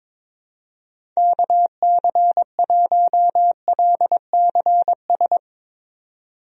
Morse code, heard as a keyed tone, about 22 wpm, 700 Hz. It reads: KC1LCS
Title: USA Call Signs - 6 Characters - 22wpm